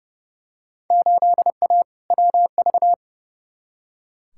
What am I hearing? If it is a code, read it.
Morse code, heard as a keyed tone, about 30 wpm, 700 Hz. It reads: 8A WV
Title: Field Day Exchanges - 30wpm